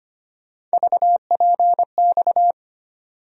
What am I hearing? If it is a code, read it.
Morse code, heard as a keyed tone, about 25 wpm, 700 Hz. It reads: VPX
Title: USA Call Sign Postfixes - 3 Letters 25wpm